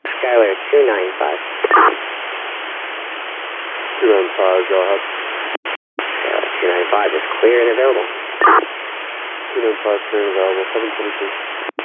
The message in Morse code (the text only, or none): none